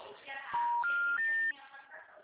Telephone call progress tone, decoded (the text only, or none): special information tone